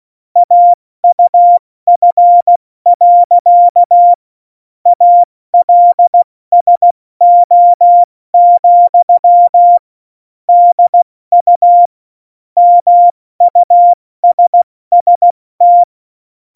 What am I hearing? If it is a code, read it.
Morse code, heard as a keyed tone, about 16 wpm, 700 Hz. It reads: AUF. ALSO, DU MUSST